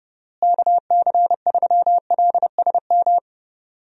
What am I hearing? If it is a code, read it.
Morse code, heard as a keyed tone, about 32 wpm, 700 Hz. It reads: KC3LSM
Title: USA Call Signs - 6 Characters - Repeated 2x - 30wpm